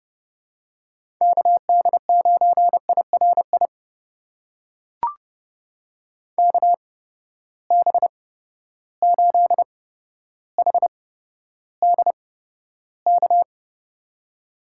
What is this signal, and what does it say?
Morse code, heard as a keyed tone, about 30 wpm, spaced as for 13 wpm, 700 Hz. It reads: KD9IRI KB8HDK
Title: USA Call Signs - 6 Characters - 8x Character Spacing - 30wpm